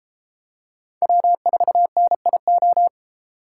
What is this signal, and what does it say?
Morse code, heard as a keyed tone, about 33 wpm, 700 Hz. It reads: W4NIO